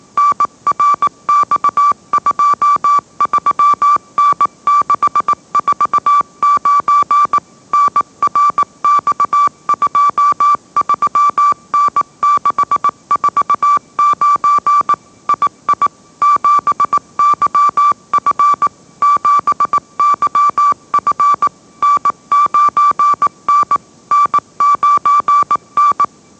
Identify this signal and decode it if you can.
Morse code, heard as a keyed tone, about 18 wpm, 1160 Hz. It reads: NRX23N649 NRX23N649 II 7YF 7YF N9N N9N
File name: mrx / mrx082 20wpm